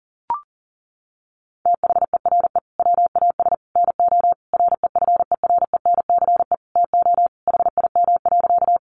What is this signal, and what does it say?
Morse code, heard as a keyed tone, about 40 wpm, 700 Hz. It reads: THERE WAS NO REFERENCE TO HIM.
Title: Sentences from Top 1200 Words 40wpm